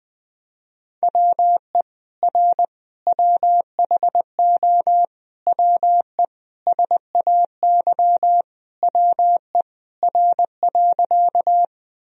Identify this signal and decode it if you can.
Morse code, heard as a keyed tone, about 20 wpm, 700 Hz. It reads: WE R WHO WE SAY WE R.
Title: Sentences from Top 200 Words and Abbreviations 20wpm